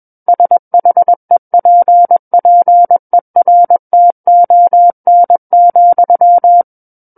Morse code, heard as a keyed tone, about 21 wpm, 700 Hz. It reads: SHEPPERTON,
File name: Book One / 12-Chapter 12 - 21 WPM